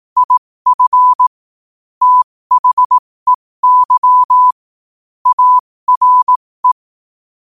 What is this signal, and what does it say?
Morse code, heard as a keyed tone, about 18 wpm, spaced as for 15 wpm, 1000 Hz. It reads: IF THEY ARE
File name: qotdfiles / 15wpm